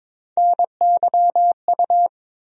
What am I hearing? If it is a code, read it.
Morse code, heard as a keyed tone, about 22 wpm, 700 Hz. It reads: NYU
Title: USA Call Sign Postfixes - 3 Letters 22wpm